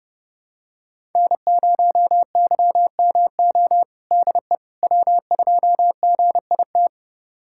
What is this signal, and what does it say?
Morse code, heard as a keyed tone, about 32 wpm, 700 Hz. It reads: N0YMO DE W2GIT
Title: USA Callsign DE Callsign - 5 Characters - 30wpm